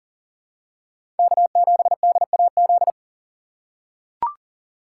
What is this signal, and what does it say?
Morse code, heard as a keyed tone, about 40 wpm, 700 Hz. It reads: K7DAZ
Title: USA Call Signs - 5 Characters - 3x Character Spacing - 40wpm